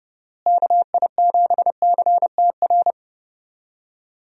Morse code, heard as a keyed tone, about 30 wpm, 700 Hz. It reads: KI7CTR
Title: USA Call Signs - 6 Characters - Repeated 2x - 30wpm